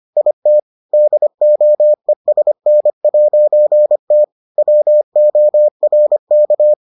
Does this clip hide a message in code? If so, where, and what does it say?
Yes: Morse code from 0.2 to 6.7 s, heard as a keyed tone, about 25 wpm, 600 Hz. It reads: IT DOESN'T WORK